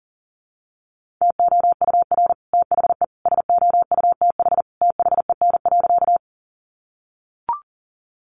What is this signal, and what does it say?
Morse code, heard as a keyed tone, about 37 wpm, 700 Hz. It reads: TOUR THE SOUTH THEN.
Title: Sentences from Top 1400 Words 40wpm